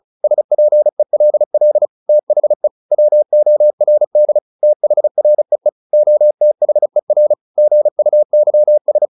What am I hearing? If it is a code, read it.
Morse code, heard as a keyed tone, about 34 wpm, 600 Hz. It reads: SPELL THE WORD THREE OTHER GUYS